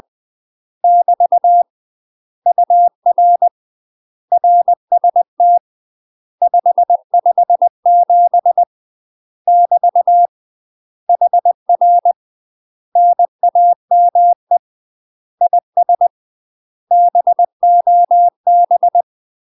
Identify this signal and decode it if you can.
Morse code, heard as a keyed tone, about 20 wpm, 700 Hz. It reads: <BT> UR RST 557 <BT> HR NAME IS BOB
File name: UK MASTER QSOs TRAINING / 20WPM / UK MASTER QSOs 0040